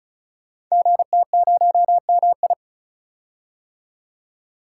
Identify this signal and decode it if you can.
Morse code, heard as a keyed tone, about 35 wpm, 700 Hz. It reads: GT0MI
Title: Code Groups - 5 Letter:Numbers - 35wpm